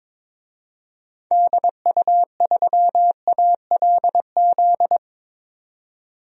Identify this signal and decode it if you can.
Morse code, heard as a keyed tone, about 22 wpm, 700 Hz. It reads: DU3ALZ
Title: DXCC-100 Call Signs - 6 Characters - 22wpm